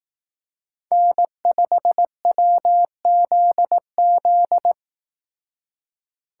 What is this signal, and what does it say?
Morse code, heard as a keyed tone, about 18 wpm, 700 Hz. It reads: N5WZZ
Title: Speed Racing - USA 1x3 Call Signs 15wpm